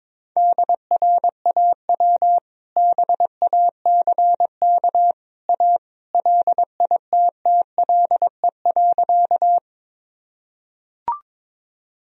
Morse code, heard as a keyed tone, about 22 wpm, 700 Hz. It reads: DRAW BACK A LITTLE.
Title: Sentences from Top 1000 Words - Review Part 1 - 22wpm